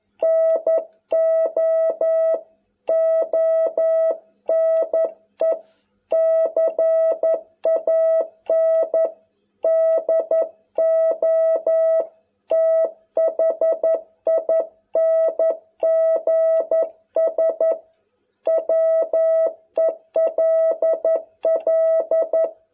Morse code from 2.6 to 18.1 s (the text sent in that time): ONE CAN DO THINGS